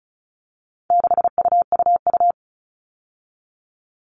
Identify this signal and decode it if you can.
Morse code, heard as a keyed tone, about 34 wpm, 700 Hz. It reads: 6UUU
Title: All Cut Numbers - 4-Digit 35wpm